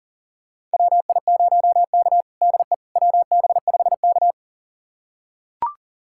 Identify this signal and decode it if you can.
Morse code, heard as a keyed tone, about 40 wpm, 700 Hz. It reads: WI0K DE WB5K
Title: USA Callsign DE Callsign - 4 Characters - 40wpm